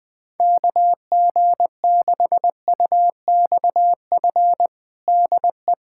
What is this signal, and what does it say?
Morse code, heard as a keyed tone, about 21 wpm, 700 Hz. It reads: KG6UXF DE
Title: USA Callsign DE Callsign 20wpm